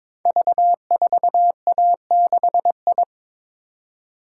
Morse code, heard as a keyed tone, about 22 wpm, 700 Hz. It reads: V4A6I
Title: Code Groups - 5 Letter:Numbers - 22wpm